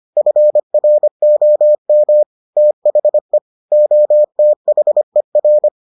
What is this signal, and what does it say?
Morse code, heard as a keyed tone, about 25 wpm, 600 Hz. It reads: FROM THE OTHER